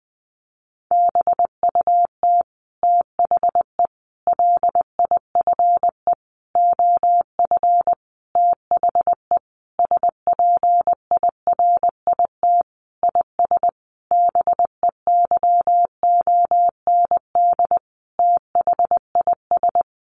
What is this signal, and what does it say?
Morse code, heard as a keyed tone, about 20 wpm, 700 Hz. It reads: BUT THE LIFE OF THE SPIRIT IS BEYOND THIS
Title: Sentences from Top 1500 Words 20wpm